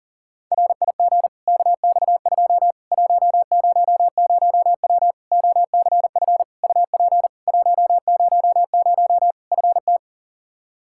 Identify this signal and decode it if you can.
Morse code, heard as a keyed tone, about 40 wpm, 700 Hz. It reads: RIG KX2 100W OCF UP 100 FT
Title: Practice QSO - Exchange 2 - Info - 40wpm